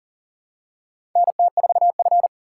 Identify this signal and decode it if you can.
Morse code, heard as a keyed tone, about 40 wpm, 700 Hz. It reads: NT4F
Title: USA Call Signs - 4 Characters - 4x Character Spacing - 40wpm